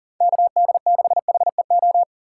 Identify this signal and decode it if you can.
Morse code, heard as a keyed tone, about 40 wpm, 700 Hz. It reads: KD6HEO
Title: USA Call Signs - 6 Characters - 4x Character Spacing - 40wpm